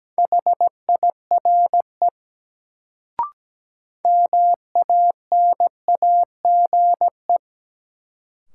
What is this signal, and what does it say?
Morse code, heard as a keyed tone, about 17 wpm, 700 Hz. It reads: HIRE MANAGE
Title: Sentences from Top 1400 Words - New Word Drill - 17wpm